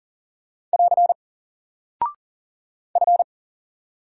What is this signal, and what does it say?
Morse code, heard as a keyed tone, about 40 wpm, 700 Hz. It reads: <AR> F
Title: Single Character (Letter:Number:Punctuation:Prosign) Alt 40wpm